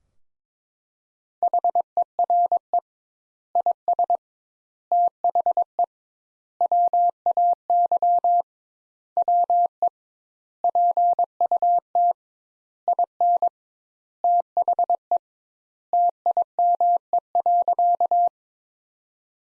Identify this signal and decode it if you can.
Morse code, heard as a keyed tone, about 22 wpm, 700 Hz. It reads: HERE IS THE WAY WE PUT IN THE TIME.
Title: Sentences Top 200 Words - 22wpm